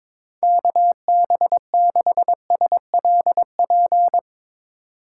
Morse code, heard as a keyed tone, about 22 wpm, 700 Hz. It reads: KB6SLP
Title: USA Call Signs - 6 Characters - 2x Character Spacing - 22wpm